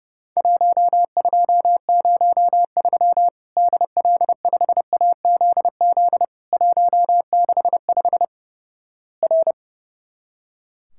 Morse code, heard as a keyed tone, about 30 wpm, 700 Hz. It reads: 1203 DL5AZZ 165 R
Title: Worked All Europe DX Contest - QTC Exchange - 30wpm